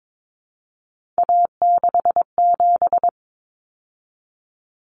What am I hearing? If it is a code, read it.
Morse code, heard as a keyed tone, about 22 wpm, 700 Hz. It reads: A67
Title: Common Cut Numbers - 3-Digit 22wpm